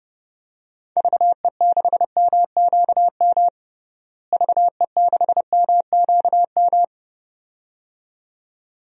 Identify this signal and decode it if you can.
Morse code, heard as a keyed tone, about 30 wpm, 700 Hz. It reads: VE6MQM VE6MQM
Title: DXCC-100 Call Signs - 6 Characters - Repeated 2x - 30wpm